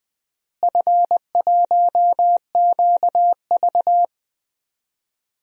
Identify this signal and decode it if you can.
Morse code, heard as a keyed tone, about 20 wpm, 700 Hz. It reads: F1QV
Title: DXCC-100 Call Signs - 4 Characters - Repeated 3x - 20wpm